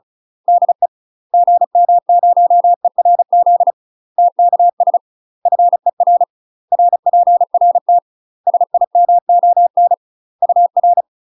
Morse code, heard as a keyed tone, about 34 wpm, 700 Hz. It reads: DE GM0ERZ TKS FER RPRT SIMON UR
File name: UK MASTER QSOs TRAINING / 35WPM / UK MASTER QSOs 0025